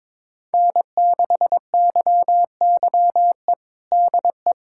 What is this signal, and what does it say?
Morse code, heard as a keyed tone, about 22 wpm, 700 Hz. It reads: N6YYE DE
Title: USA Callsign DE Callsign - 5 Characters - 22wpm